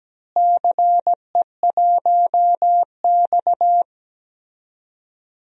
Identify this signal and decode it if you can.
Morse code, heard as a keyed tone, about 17 wpm, 700 Hz. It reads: CE1X
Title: DXCC-100 Call Signs - 4 Characters - Repeated 2x - 17wpm